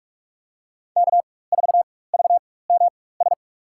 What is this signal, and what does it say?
Morse code, heard as a keyed tone, about 46 wpm, spaced as for 21 wpm, 700 Hz. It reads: K4VMS
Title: USA Call Signs - 5 Characters - 4x Character Spacing - 45wpm